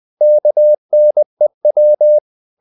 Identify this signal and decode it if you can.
Morse code, heard as a keyed tone, about 20 wpm, 600 Hz. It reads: KNEW
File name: the old man and the sea part 2 20wpm